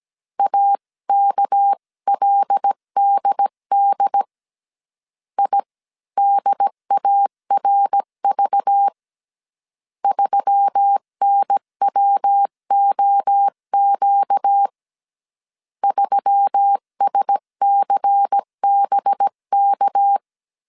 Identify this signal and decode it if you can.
Morse code, heard as a keyed tone, about 17 wpm, 800 Hz. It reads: AKLDD IDARV 3NWOQ 3SCBK
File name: CW / DET / 1 6 1500C S C 16wpm 2733